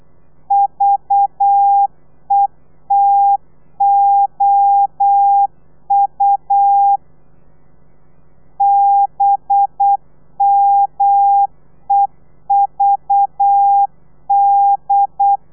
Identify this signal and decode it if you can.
Morse code, heard as a keyed tone, about 8 wpm, 800 Hz. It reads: VETOU BMEVD
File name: telegrafia / i2tfj / Lez4 35